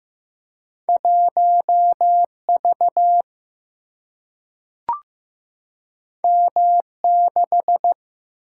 Morse code, heard as a keyed tone, about 15 wpm, 700 Hz. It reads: 1V M6
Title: Single-Letter Single-Number 15wpm